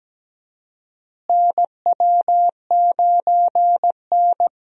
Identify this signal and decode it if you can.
Morse code, heard as a keyed tone, about 17 wpm, 700 Hz. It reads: NW9N
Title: USA Call Signs - 4 Characters - Repeated 3x - 17wpm